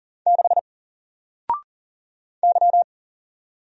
Morse code, heard as a keyed tone, about 40 wpm, 700 Hz. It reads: 6 Y
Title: Single Letter-Number 40wpm